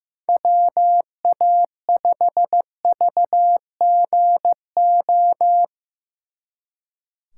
Morse code, heard as a keyed tone, about 20 wpm, 700 Hz. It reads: WA5VGO